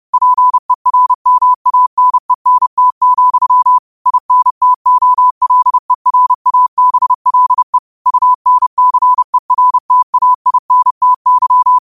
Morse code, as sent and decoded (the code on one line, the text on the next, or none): .--. .-. -- .- -. -. - --..-- / .. -. - --- .-.. .-. .- -... .-.. / ..- -. -.-. .-. - .- .. -. - -.--
PERMANENT, INTOLERABLE UNCERTAINTY